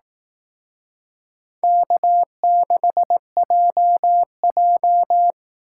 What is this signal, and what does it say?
Morse code, heard as a keyed tone, about 18 wpm, 700 Hz. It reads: K6JJ
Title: Speed Racing - USA Call Signs 15wpm